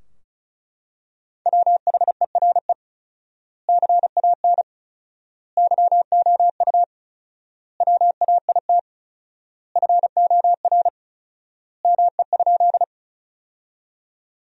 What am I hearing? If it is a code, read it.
Morse code, heard as a keyed tone, about 36 wpm, 700 Hz. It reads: WHERE CAN YOU WAIT FOR ME?
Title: 4x Word-Spacing - Sentences from Top 500 Words - 35wpm